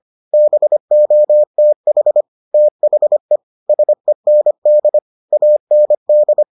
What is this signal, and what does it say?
Morse code, heard as a keyed tone, about 25 wpm, 600 Hz. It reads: BOTH THE SEND AND